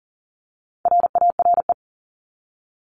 Morse code, heard as a keyed tone, about 40 wpm, 700 Hz. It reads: RARE
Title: LY Suffix - Top 100 Words with Base Words 40wpm